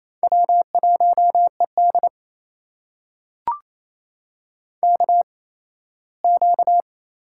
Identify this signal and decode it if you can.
Morse code, heard as a keyed tone, about 28 wpm, spaced as for 17 wpm, 700 Hz. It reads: W1ED KQ
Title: USA Call Signs - 4 Characters - 8x Character Spacing - 28wpm